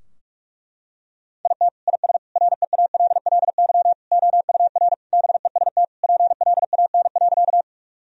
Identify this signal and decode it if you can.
Morse code, heard as a keyed tone, about 41 wpm, 700 Hz. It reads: IT IS REALLY OUR BEST PLAN.
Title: Sentences from Top 1000 Words - Review Part 2 - 45wpm